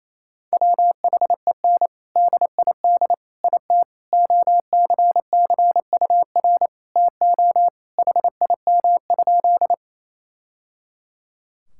Morse code, heard as a keyed tone, about 28 wpm, 700 Hz. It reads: WHEN DID IT OCCUR TO HIM?
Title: Sentences from Top 1000 Words - Review Part 6 - 28wpm